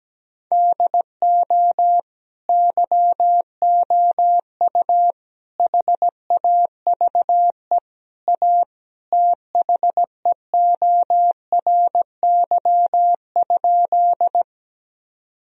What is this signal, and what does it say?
Morse code, heard as a keyed tone, about 17 wpm, 700 Hz. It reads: DO YOU HAVE A THEORY?
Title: Sentences from Top 1000 Words - Review Part 3 - 17wpm